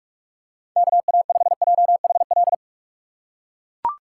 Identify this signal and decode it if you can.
Morse code, heard as a keyed tone, about 43 wpm, 700 Hz. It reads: KA5JHL